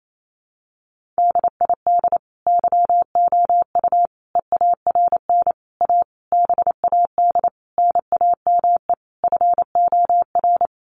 Morse code, heard as a keyed tone, about 28 wpm, 700 Hz. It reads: DID YOU EARN A BAD NAME FOR
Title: Sentences from Top 1500 Words 28wpm